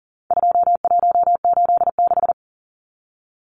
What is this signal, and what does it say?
Morse code, heard as a keyed tone, about 40 wpm, 700 Hz. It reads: 2186
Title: Numbers - 4-Digit 40wpm